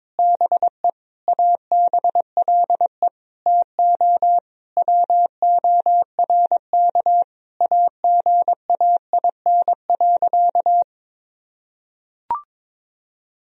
Morse code, heard as a keyed tone, about 22 wpm, 700 Hz. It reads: BE ABLE TO WORK AGAIN.
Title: Sentences from Top 1000 Words - Review Part 2 - 22wpm